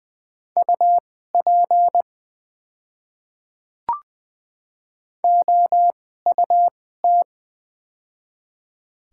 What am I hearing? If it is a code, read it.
Morse code, heard as a keyed tone, about 20 wpm, spaced as for 14 wpm, 700 Hz. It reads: UP OUT
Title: 2x Character Spacing - Top 300 Words - 20wpm